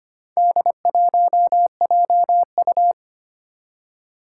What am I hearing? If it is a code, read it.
Morse code, heard as a keyed tone, about 25 wpm, 700 Hz. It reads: D1JU
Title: Four Challenging Letter-Number Combinations 25wpm